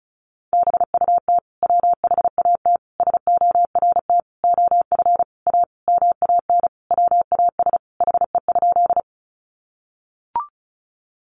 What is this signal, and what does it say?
Morse code, heard as a keyed tone, about 34 wpm, 700 Hz. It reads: BUT WHAT SORT OF A MAN WAS HE?
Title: Sentences from Top 1300 Words 35wpm